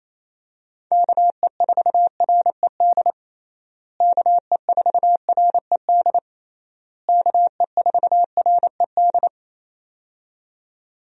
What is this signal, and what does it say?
Morse code, heard as a keyed tone, about 27 wpm, 700 Hz. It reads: KE4RED KE4RED KE4RED
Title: USA Call Signs - 6 Characters - Repeated 3x - 28wpm